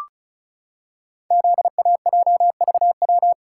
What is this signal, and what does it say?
Morse code, heard as a keyed tone, about 34 wpm, 700 Hz. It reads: ZAJVW
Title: Code Groups - 5 Letters - 35wpm